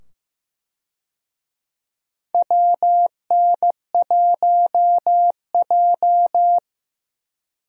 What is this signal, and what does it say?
Morse code, heard as a keyed tone, about 16 wpm, 700 Hz. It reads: WN1J
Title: USA Call Signs - 4 Characters - 15wpm